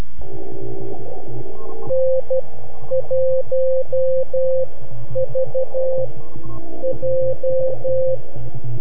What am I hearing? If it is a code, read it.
Morse code, heard as a keyed tone, about 12 wpm, spaced as for 9 wpm, 520 Hz. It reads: N1VJ
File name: NAQPCW-8-2009 / W1UJ NAQP CW 09 0100Z